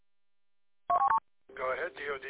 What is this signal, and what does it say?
Speech. There is no code.